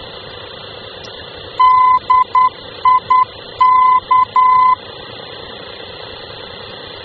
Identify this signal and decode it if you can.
Morse code, heard as a keyed tone, about 10 wpm, 1020 Hz. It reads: DIK
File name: luxembourg / mx / DIK-L-VOR